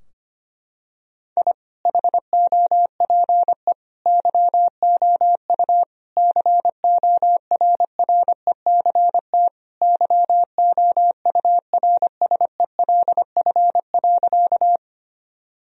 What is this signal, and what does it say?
Morse code, heard as a keyed tone, about 25 wpm, 700 Hz. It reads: I HOPE YOU CORRECT YOURSELF.
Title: Sentences from Top 1000 Words - Review Part 2 - 25wpm